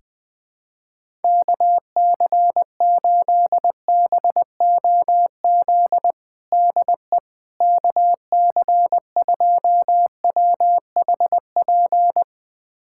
Morse code, heard as a keyed tone, about 21 wpm, 700 Hz. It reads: KC8BOZ DE KC2WHP
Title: USA Callsign DE Callsign - 6 Characters - 20wpm